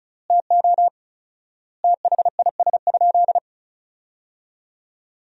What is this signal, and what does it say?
Morse code, heard as a keyed tone, about 36 wpm, 700 Hz. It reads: TO THIS?
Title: 4x Word-Spacing - Sentences from Top 100 Words - 35wpm